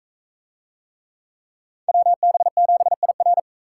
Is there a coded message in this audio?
Yes: Morse code from 1.9 to 3.4 s, heard as a keyed tone, about 39 wpm, 700 Hz. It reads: WB7IR